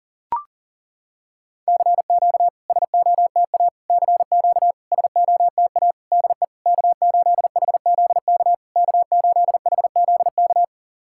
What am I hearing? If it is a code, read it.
Morse code, heard as a keyed tone, about 38 wpm, 700 Hz. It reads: CQ SOTA CQ SOTA DE K8HZK K8HZK
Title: SOTA QSO Practice 40wpm